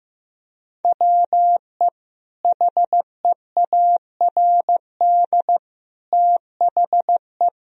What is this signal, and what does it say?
Morse code, heard as a keyed tone, about 16 wpm, 700 Hz. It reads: WE HEARD THE
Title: Sentences from Top 1000 Words - Review Part 2 - 15wpm